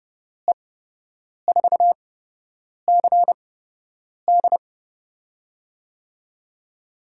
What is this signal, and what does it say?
Morse code, heard as a keyed tone, about 30 wpm, spaced as for 8 wpm, 700 Hz. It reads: E4CD
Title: USA Call Signs - 8x Character Spacing - 30wpm